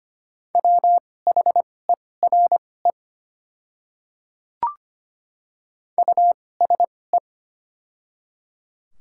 Morse code, heard as a keyed tone, about 25 wpm, spaced as for 18 wpm, 700 Hz. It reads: WHERE USE
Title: 2x Character Spacing - Top 200 Words - 25wpm